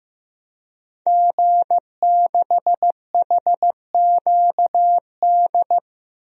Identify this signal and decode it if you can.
Morse code, heard as a keyed tone, about 15 wpm, 700 Hz. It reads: G6HQD
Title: DXCC-100 Call Signs - 5 Characters - Repeated 3x - 15wpm